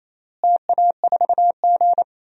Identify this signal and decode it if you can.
Morse code, heard as a keyed tone, about 28 wpm, 700 Hz. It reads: TA4G
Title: DXCC-100 Call Signs - 4 Characters - Repeated 3x - 28wpm